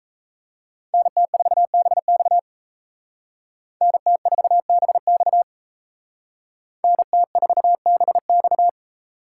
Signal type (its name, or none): Morse code